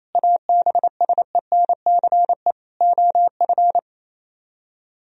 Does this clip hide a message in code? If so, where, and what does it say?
Yes: Morse code from 0.2 to 3.8 s, heard as a keyed tone, about 37 wpm, 700 Hz. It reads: ABSENCE OF